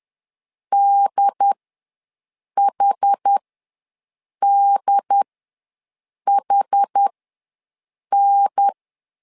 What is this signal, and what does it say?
Morse code, heard as a keyed tone, about 10 wpm, spaced as for 6 wpm, 800 Hz. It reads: DHDHN
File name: LEZIONI / 3 MWRO / 3 M ws 10wpm 2121